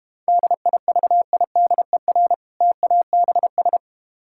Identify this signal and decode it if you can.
Morse code, heard as a keyed tone, about 32 wpm, 700 Hz. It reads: DIVIDER TABS